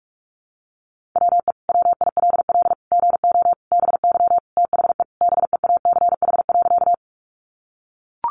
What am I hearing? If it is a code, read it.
Morse code, heard as a keyed tone, about 44 wpm, 700 Hz. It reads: WE WILL GO BY THE BEACH.
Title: Sentences from Top 1500 Words 45wpm